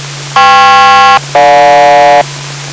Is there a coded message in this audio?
Yes: aircraft SELCAL from 0.4 to 2.2 s, heard as two pairs of tones. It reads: KP-GJ